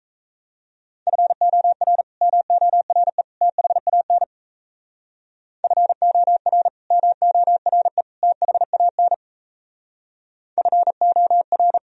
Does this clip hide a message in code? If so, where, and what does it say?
Yes: Morse code from 1.1 to 11.8 s, heard as a keyed tone, about 35 wpm, 700 Hz. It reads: FOR MORE THAN FOR MORE THAN FOR